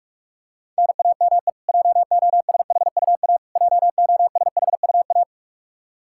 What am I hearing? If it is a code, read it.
Morse code, heard as a keyed tone, about 46 wpm, 700 Hz. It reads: NAME JOSHUA JOSHUA
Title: Practice QSO - Exchange 1 - Beginning - 45wpm